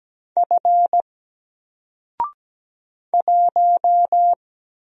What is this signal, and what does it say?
Morse code, heard as a keyed tone, about 17 wpm, 700 Hz. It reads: F 1
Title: Single Letter-Number 17wpm